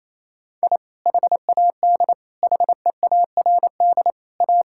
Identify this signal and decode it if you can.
Morse code, heard as a keyed tone, about 28 wpm, 700 Hz. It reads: I HAD HEARD A
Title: Sentences from Top 1000 Words - Review Part 6 - 28wpm